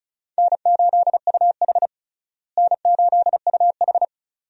Morse code, heard as a keyed tone, about 34 wpm, 700 Hz. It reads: N8UH N8UH
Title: USA Call Signs - 4 Characters - Repeated 2x - 35wpm